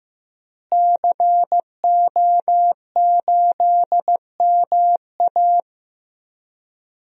Morse code, heard as a keyed tone, about 16 wpm, 700 Hz. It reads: CO8MA
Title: DXCC-100 Call Signs - 5 Characters - Repeated 2x - 15wpm